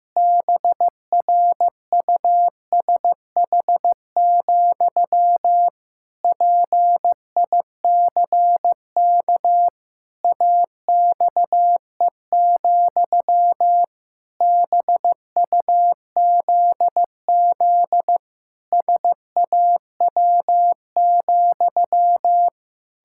Morse code, heard as a keyed tone, about 15 wpm, 700 Hz. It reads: BRUSH, PICK AXE, BUZZ SAW,
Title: Pangram Challenge v2 - 15wpm